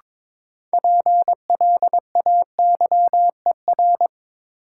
Morse code, heard as a keyed tone, about 22 wpm, 700 Hz. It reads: PLAYER
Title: Sentences from Top 1000 Words - Review Part 4 - New Word Drill - 22wpm